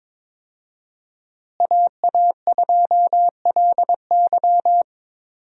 Morse code, heard as a keyed tone, about 22 wpm, 700 Hz. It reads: AA2LY
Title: USA Call Signs - 5 Characters - Repeated 3x - 22wpm